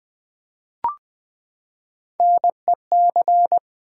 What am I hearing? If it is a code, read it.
Morse code, heard as a keyed tone, about 20 wpm, 700 Hz. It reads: NEC
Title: Build A Word - 3 Letters At A Time - 9-Letter Words - 20wpm